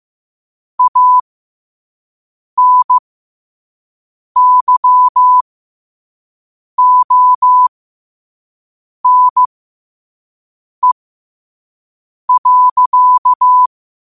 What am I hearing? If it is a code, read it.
Morse code, heard as a keyed tone, about 15 wpm, spaced as for 5 wpm, 1000 Hz. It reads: ANYONE.
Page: New quotes every day in morse code at 5 Words per minute.